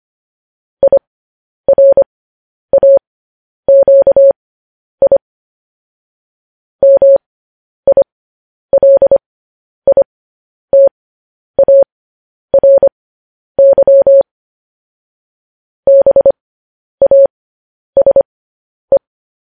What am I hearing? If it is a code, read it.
Morse code, heard as a keyed tone, about 25 wpm, spaced as for 10 wpm, 550 Hz. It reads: IRAQI MILITARY BASE